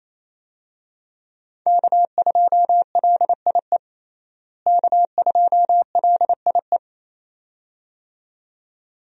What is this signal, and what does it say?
Morse code, heard as a keyed tone, about 27 wpm, 700 Hz. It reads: K2LIE K2LIE
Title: USA Call Signs - 5 Characters - Repeated 2x - 28wpm